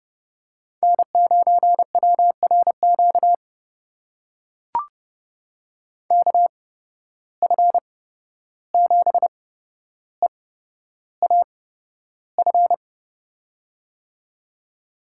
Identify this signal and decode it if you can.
Morse code, heard as a keyed tone, about 30 wpm, spaced as for 12 wpm, 700 Hz. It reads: N9WRQ KF7EAF
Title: USA Call Signs - 8x Character Spacing - 30wpm